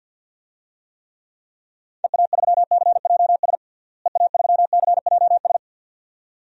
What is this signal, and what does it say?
Morse code, heard as a keyed tone, about 51 wpm, 700 Hz. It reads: EA3CJS EA3CJS
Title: DXCC-100 Call Signs - 6 Characters - Repeated 2x - 50wpm